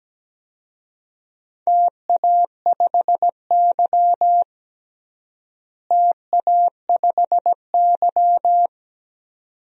Morse code, heard as a keyed tone, about 17 wpm, 700 Hz. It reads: TA5Y TA5Y
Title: DXCC-100 Call Signs - 4 Characters - Repeated 2x - 17wpm